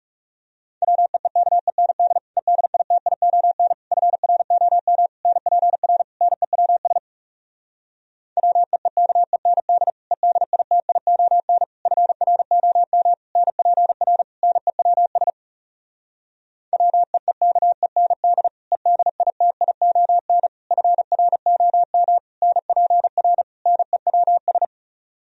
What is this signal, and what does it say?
Morse code, heard as a keyed tone, about 40 wpm, 700 Hz. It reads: WEEKEND EDITION FROM NPR NEWS WEEKEND EDITION FROM NPR NEWS WEEKEND EDITION FROM NPR NEWS